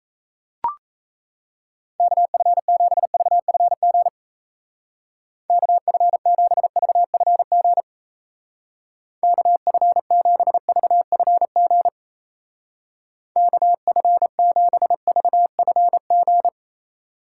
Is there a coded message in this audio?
Yes: Morse code from 2.0 to 16.5 s, heard as a keyed tone, about 35 wpm, 700 Hz. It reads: KF7VFG KF7VFG KF7VFG KF7VFG